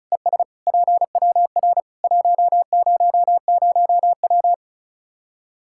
Morse code, heard as a keyed tone, about 34 wpm, 700 Hz. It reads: ES PWR 100W
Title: Practice QSO - Exchange 2 - Info - 35wpm